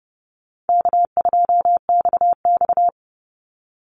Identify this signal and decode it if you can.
Morse code, heard as a keyed tone, about 30 wpm, 700 Hz. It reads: K2XX
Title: Easy Call Signs - 30wpm